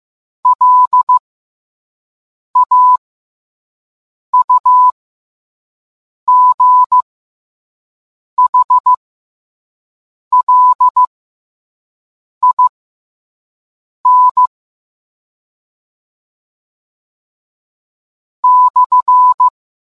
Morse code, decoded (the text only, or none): LAUGHLIN /